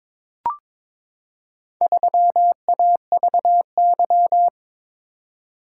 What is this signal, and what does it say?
Morse code, heard as a keyed tone, about 22 wpm, 700 Hz. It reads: 3AVY
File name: practice / 22wpm / Four Letter-Number Combinations - 22wpm